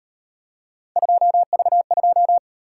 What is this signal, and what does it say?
Morse code, heard as a keyed tone, about 38 wpm, 700 Hz. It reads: 2V2